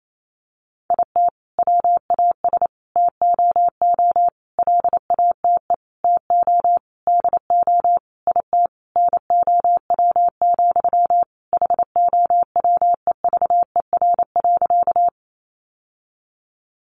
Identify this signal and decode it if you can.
Morse code, heard as a keyed tone, about 28 wpm, 700 Hz. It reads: IT WAS TOO LATE TO DO IT NOW, HOWEVER.
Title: Sentences from Top 600 Words 28wpm